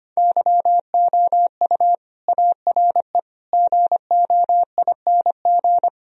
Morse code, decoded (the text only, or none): YOU ARE GOING